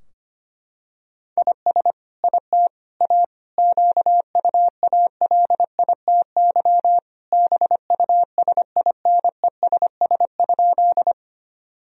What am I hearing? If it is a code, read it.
Morse code, heard as a keyed tone, about 25 wpm, 700 Hz. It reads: IS IT A QUALITY BUSINESS?
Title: Sentences from Top 1000 Words - Review Part 4 - 25wpm